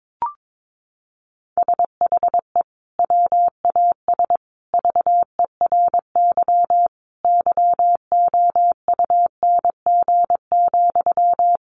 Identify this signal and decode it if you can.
Morse code, heard as a keyed tone, about 22 wpm, 700 Hz. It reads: SHE WAS VERY YOUNG,
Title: Sentences from Top 1400 Words 22wpm